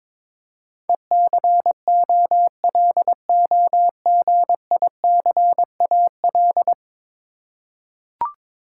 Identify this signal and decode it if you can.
Morse code, heard as a keyed tone, about 22 wpm, 700 Hz. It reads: ECOLOGICAL
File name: practice / 22wpm / 10 Letter Words - First 3 Letters Given - 22wpm